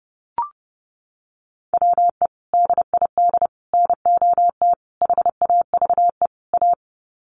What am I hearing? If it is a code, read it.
Morse code, heard as a keyed tone, about 30 wpm, 700 Hz. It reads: WE DID NOT HAVE A
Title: Sets of 5 Words Encore2 30wpm